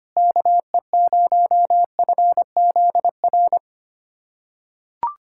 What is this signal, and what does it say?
Morse code, heard as a keyed tone, about 25 wpm, 700 Hz. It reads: KE0FZR
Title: USA Callsign DE Callsign - 6 Characters - 25wpm